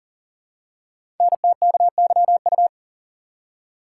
Morse code, heard as a keyed tone, about 40 wpm, 700 Hz. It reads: NTKYU
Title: Code Groups - 5 Letters - 40wpm